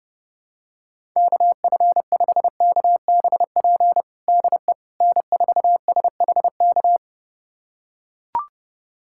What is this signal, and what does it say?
Morse code, heard as a keyed tone, about 32 wpm, 700 Hz. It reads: KF5KBP DE N4SHK
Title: USA Callsign DE Callsign 30wpm